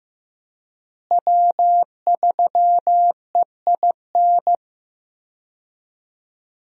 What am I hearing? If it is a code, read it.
Morse code, heard as a keyed tone, about 15 wpm, 700 Hz. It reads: W3EIN
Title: USA Call Signs - 5 Characters - Repeated 3x - 15wpm